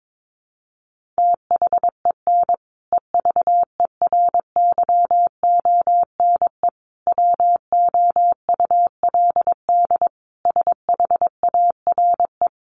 Morse code, heard as a keyed tone, about 22 wpm, 700 Hz. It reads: THEN EVERYONE WOULD SHARE
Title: Sentences from Top 1400 Words 22wpm